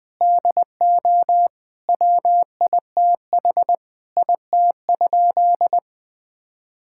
Morse code, heard as a keyed tone, about 20 wpm, 700 Hz. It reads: DO WITH IT?
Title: Sentences from Top 1600 Words 20wpm